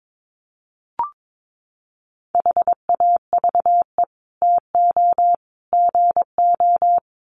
Morse code, heard as a keyed tone, about 22 wpm, 700 Hz. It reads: HAVE TO GO
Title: Sets of 3 Words 22wpm